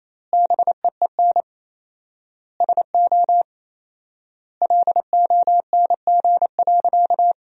Morse code, heard as a keyed tone, about 28 wpm, 700 Hz. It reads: BEEN SO LONG.
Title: 4x Word-Spacing - Sentences from Top 100 Words - 28wpm